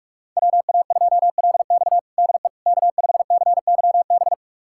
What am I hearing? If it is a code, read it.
Morse code, heard as a keyed tone, about 46 wpm, 700 Hz. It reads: WA2LX DE K5CYB
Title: USA Callsign DE Callsign - 5 Characters - 45wpm